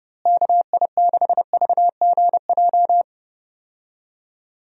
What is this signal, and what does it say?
Morse code, heard as a keyed tone, about 32 wpm, 700 Hz. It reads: KI6VGJ
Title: USA Call Signs - 6 Characters - Repeated 2x - 30wpm